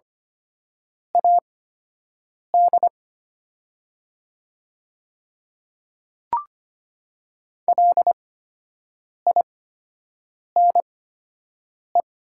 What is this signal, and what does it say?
Morse code, heard as a keyed tone, about 25 wpm, spaced as for 7 wpm, 700 Hz. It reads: AD LINE
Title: 8x Character Spacing - Top 200 Words - 25wpm